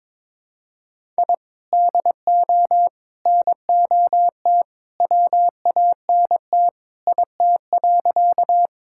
Morse code, heard as a keyed tone, about 21 wpm, 700 Hz. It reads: I DO NOT WANT IT.
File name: practice / 22wpm / Sentences with 2-Word Phrases - 22wpm